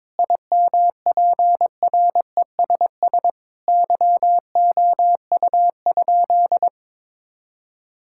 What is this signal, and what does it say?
Morse code, heard as a keyed tone, about 22 wpm, 700 Hz. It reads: IMPRESS YOU?
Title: Sentences from Top 1600 Words 22wpm